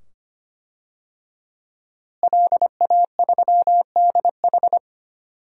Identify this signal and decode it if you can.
Morse code, heard as a keyed tone, about 25 wpm, 700 Hz. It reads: LA3DH
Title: DXCC-100 Call Signs - 5 Characters - Repeated 2x - 25wpm